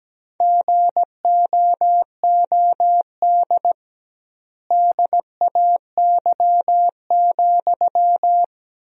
Morse code, heard as a keyed tone, about 17 wpm, 700 Hz. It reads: GOOD DAY,
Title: 2x Word-Spacing - Sentences from Top 200 Words - 17wpm